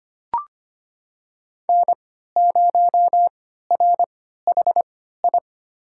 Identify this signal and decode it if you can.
Morse code, heard as a keyed tone, about 25 wpm, spaced as for 14 wpm, 700 Hz. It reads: N0RHI
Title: USA Call Signs - 5 Characters - 3x Character Spacing - 25wpm